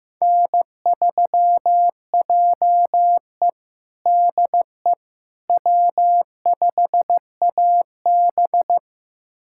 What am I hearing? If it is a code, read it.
Morse code, heard as a keyed tone, about 15 wpm, 700 Hz. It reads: N3JE DE W5AB
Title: USA Callsign DE Callsign - 4 Characters - 15wpm